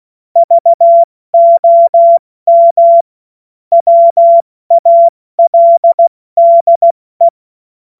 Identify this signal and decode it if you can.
Morse code, heard as a keyed tone, about 16 wpm, 680 Hz. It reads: VOM WALDE
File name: Morsekurs / Witze / Witz 31. Januar 2023 70 F80